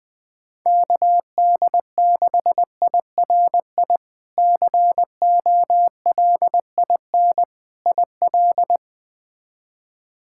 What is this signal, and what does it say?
Morse code, heard as a keyed tone, about 21 wpm, 700 Hz. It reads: KD6IRI COLIN IL
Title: K1USN SST QSO Practice - 20wpm